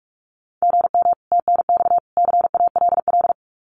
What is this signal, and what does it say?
Morse code, heard as a keyed tone, about 44 wpm, 700 Hz. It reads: GM TNX CALL
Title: Common QSO Phrases 45wpm